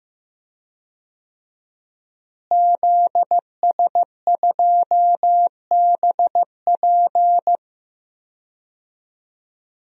Morse code, heard as a keyed tone, about 15 wpm, 700 Hz. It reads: ZS2BP
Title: DXCC-100 Call Signs - 5 Characters - Repeated 2x - 15wpm